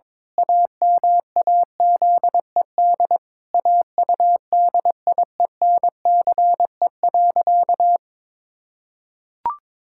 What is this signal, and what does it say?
Morse code, heard as a keyed tone, about 22 wpm, 700 Hz. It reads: AMAZED AUDIENCE.
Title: Pangram Challenge v2 - 22wpm